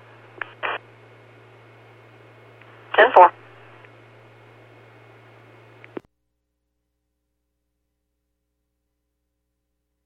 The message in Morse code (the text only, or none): none